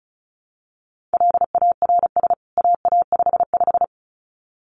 Morse code, heard as a keyed tone, about 34 wpm, 700 Hz. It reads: LARS AA55
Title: CWT Exchange 35wpm